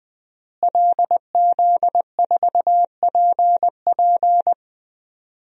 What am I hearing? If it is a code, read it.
Morse code, heard as a keyed tone, about 20 wpm, 700 Hz. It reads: LZ4PP
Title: DXCC-100 Call Signs - 5 Characters - Repeated 3x - 20wpm